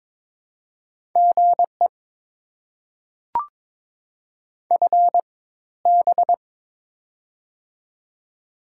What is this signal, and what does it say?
Morse code, heard as a keyed tone, about 22 wpm, spaced as for 14 wpm, 690 Hz. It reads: GE FB
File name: Top 20 QSO Elements - 4x Character Spacing - 22wpm